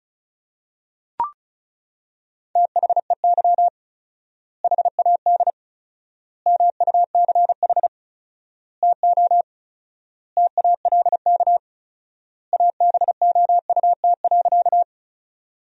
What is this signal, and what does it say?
Morse code, heard as a keyed tone, about 36 wpm, 700 Hz. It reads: THEY HAD MUCH TO TALK ABOUT.
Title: 4x Word-Spacing - Sentences from Top 400 Words - 35wpm